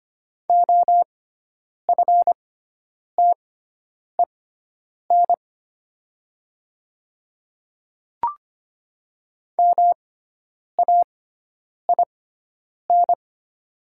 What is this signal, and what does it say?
Morse code, heard as a keyed tone, about 25 wpm, spaced as for 9 wpm, 700 Hz. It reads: OFTEN MAIN
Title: 6x Character Spacing - Top 300 Words - 25wpm